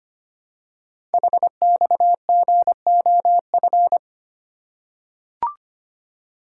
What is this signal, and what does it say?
Morse code, heard as a keyed tone, about 25 wpm, 700 Hz. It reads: HXGOF
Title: Code Groups - 5 Letters - 25wpm